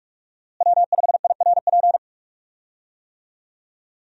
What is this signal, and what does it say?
Morse code, heard as a keyed tone, about 42 wpm, 700 Hz. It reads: W5IRP
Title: USA Call Signs - 5 Characters - Repeated 3x - 45wpm